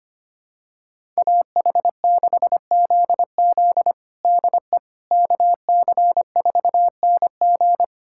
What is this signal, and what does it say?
Morse code, heard as a keyed tone, about 25 wpm, 700 Hz. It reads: AH6ZZ DE KC4NG
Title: USA Callsign DE Callsign - 5 Characters - 25wpm